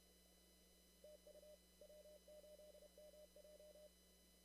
Morse code, heard as a keyed tone, about 30 wpm, 600 Hz. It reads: TU W8MJ
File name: CW / kt5j wpx cw-op-k5pi